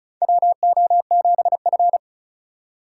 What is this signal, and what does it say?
Morse code, heard as a keyed tone, about 36 wpm, 690 Hz. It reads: WO7F
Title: USA Call Signs - 4 Characters - 3x Character Spacing - 35wpm